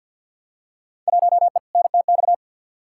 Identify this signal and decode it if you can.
Morse code, heard as a keyed tone, about 48 wpm, 700 Hz. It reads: 1E NTX
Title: Field Day Exchanges - 50wpm